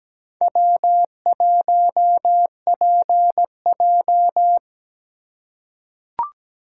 Morse code, heard as a keyed tone, about 17 wpm, 700 Hz. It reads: W1PJ
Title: USA Call Signs - 4 Characters - 3x Character Spacing - 17wpm